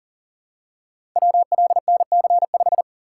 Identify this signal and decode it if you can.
Morse code, heard as a keyed tone, about 38 wpm, 700 Hz. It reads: WLNC5
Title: Code Groups - 5 Letter:Numbers - 40wpm